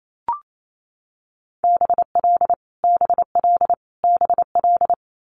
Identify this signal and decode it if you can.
Morse code, heard as a keyed tone, about 28 wpm, 700 Hz. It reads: BL BL BL
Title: Multi-letter Phonemes Repeated 3x 28wpm